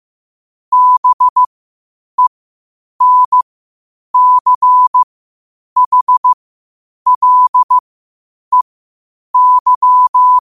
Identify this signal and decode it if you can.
Morse code, heard as a keyed tone, about 15 wpm, spaced as for 8 wpm, 1000 Hz. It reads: BENCHLEY
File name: qotdfiles / 8wpm